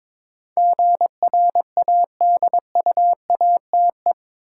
Morse code, heard as a keyed tone, about 21 wpm, 700 Hz. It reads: GRADUATE
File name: practice / 22wpm / Eight Letter Words - 22wpm